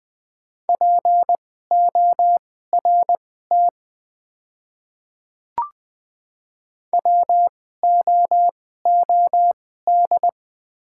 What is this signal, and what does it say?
Morse code, heard as a keyed tone, about 20 wpm, spaced as for 14 wpm, 700 Hz. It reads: PORT WOOD
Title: 2x Character Spacing - Top 300 Words - 20wpm